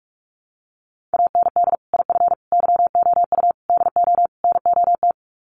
Morse code, heard as a keyed tone, about 44 wpm, 700 Hz. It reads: AND IF YOU DO NOT
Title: Sets of 5 Words 45wpm